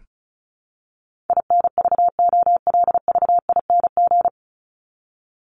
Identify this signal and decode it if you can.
Morse code, heard as a keyed tone, about 34 wpm, 700 Hz. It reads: INVOLVING
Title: ING Suffix - Top 100 Words with Base Words 35wpm